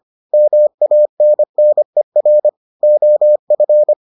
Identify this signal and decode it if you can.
Morse code, heard as a keyed tone, about 25 wpm, 600 Hz. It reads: MANNER OF